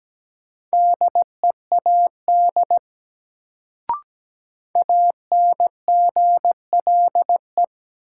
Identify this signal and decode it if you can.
Morse code, heard as a keyed tone, about 17 wpm, 700 Hz. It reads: DEAD ANGLE
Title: Sentences from Top 1500 Words - New Word Drill - 17wpm